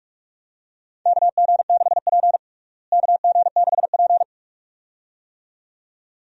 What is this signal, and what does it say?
Morse code, heard as a keyed tone, about 46 wpm, 700 Hz. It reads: KG6P KG6P
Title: USA Call Signs - 4 Characters - Repeated 2x - 45wpm